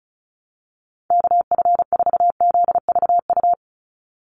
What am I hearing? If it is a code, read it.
Morse code, heard as a keyed tone, about 36 wpm, 700 Hz. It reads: KF4ZVU
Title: USA Call Signs 35wpm